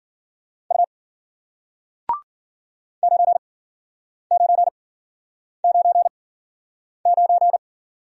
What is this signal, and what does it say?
Morse code, heard as a keyed tone, about 51 wpm, spaced as for 9 wpm, 700 Hz. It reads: U 9999